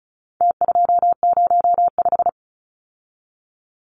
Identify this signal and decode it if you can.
Morse code, heard as a keyed tone, about 34 wpm, 700 Hz. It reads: T205
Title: Numbers - Contest Style - 4-Digit 35wpm